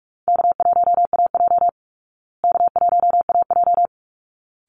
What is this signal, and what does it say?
Morse code, heard as a keyed tone, about 44 wpm, 700 Hz. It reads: K1AJ K1AJ
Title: Easy Call Signs Repeated 2x - 45wpm